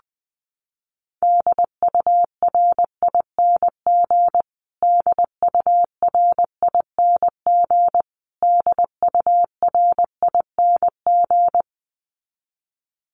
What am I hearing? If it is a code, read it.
Morse code, heard as a keyed tone, about 20 wpm, 700 Hz. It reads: DURING DURING DURING
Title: Top 500 Words Repeated 20wpm